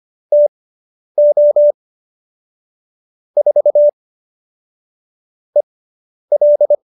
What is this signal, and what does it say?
Morse code, heard as a keyed tone, about 25 wpm, spaced as for 10 wpm, 590 Hz. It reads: TO 4 EL
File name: QSO220 10